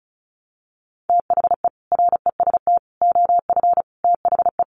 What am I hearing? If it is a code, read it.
Morse code, heard as a keyed tone, about 34 wpm, 700 Hz. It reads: THE REST OF THE
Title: Sets of 5 Words 35wpm